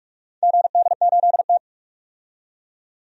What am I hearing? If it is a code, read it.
Morse code, heard as a keyed tone, about 46 wpm, 700 Hz. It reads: GD8T
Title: DXCC-100 Call Signs - 4 Characters - 45wpm